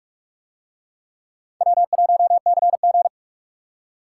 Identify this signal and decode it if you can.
Morse code, heard as a keyed tone, about 46 wpm, 700 Hz. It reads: W1CG